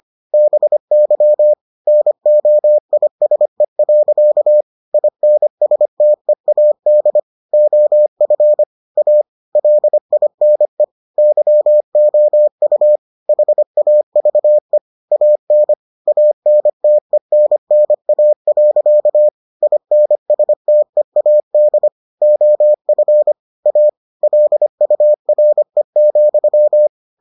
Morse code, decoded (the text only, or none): BY NOISE. INSTEAD OF A LINE YOU HAVE AN ANTENNA. INSTEAD OF A LURE,